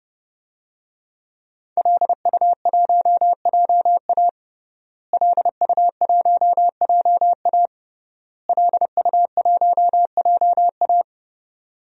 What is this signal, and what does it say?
Morse code, heard as a keyed tone, about 30 wpm, 700 Hz. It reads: LU1JA LU1JA LU1JA
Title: DXCC-100 Call Signs - 5 Characters - Repeated 3x - 30wpm